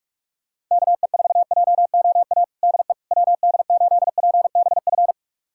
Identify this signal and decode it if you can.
Morse code, heard as a keyed tone, about 46 wpm, 700 Hz. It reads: KE4JOA DE WD8PBF
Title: USA Callsign DE Callsign - 6 Characters - 45wpm